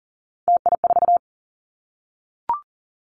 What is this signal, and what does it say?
Morse code, heard as a keyed tone, about 40 wpm, 700 Hz. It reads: TI4
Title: Callsign Prefixes for Top 100 DXCC - 40wpm